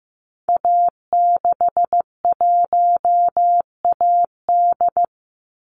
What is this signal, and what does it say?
Morse code, heard as a keyed tone, about 15 wpm, 700 Hz. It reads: A61AD
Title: Call Signs Repeated 3x - 15wpm